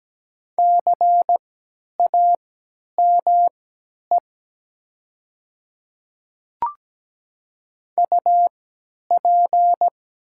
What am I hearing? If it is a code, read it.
Morse code, heard as a keyed tone, about 17 wpm, spaced as for 10 wpm, 700 Hz. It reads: CAME UP
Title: 3x Character Spacing - Top 500 Words - 17wpm